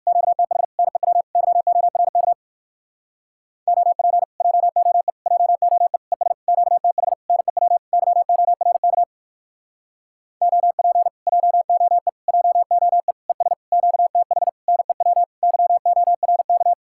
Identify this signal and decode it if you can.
Morse code, heard as a keyed tone, about 58 wpm, 700 Hz. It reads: QTH NEW YORK OP JOE JOE ES QTH NEW YORK OP JOE JOE ES QTH NEW YORK